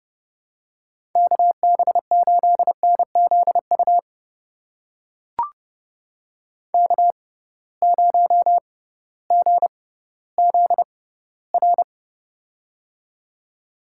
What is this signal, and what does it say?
Morse code, heard as a keyed tone, about 30 wpm, spaced as for 16 wpm, 700 Hz. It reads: KB8NZU K0GZR
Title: USA Call Signs - 6x Character Spacing - 30wpm